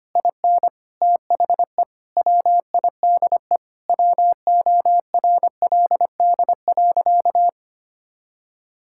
Morse code, heard as a keyed tone, about 25 wpm, 700 Hz. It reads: IN THE WIDE WORLD.
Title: Sentences from Top 1000 Words - Review Part 6 - 25wpm